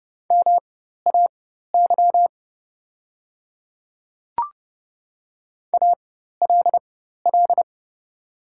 Morse code, heard as a keyed tone, about 31 wpm, spaced as for 14 wpm, 700 Hz. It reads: MAY ALL
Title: 4x Character Spacing - Top 100 Words - 30wpm